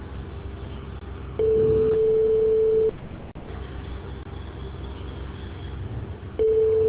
Ringback tone (France), 440 Hz.